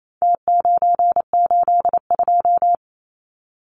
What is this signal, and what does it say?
Morse code, heard as a keyed tone, about 28 wpm, 700 Hz. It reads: T982
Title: Numbers - Contest Style - 4-Digit 28wpm